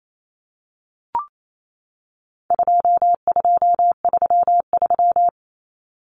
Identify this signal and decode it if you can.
Morse code, heard as a keyed tone, about 28 wpm, 700 Hz. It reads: 2233
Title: Common Cut Numbers - 4-Digit 28wpm